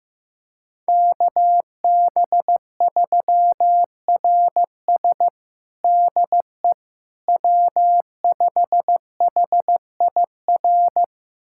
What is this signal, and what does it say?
Morse code, heard as a keyed tone, about 15 wpm, 700 Hz. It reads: KB3RS DE W5HIR
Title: USA Callsign DE Callsign 15wpm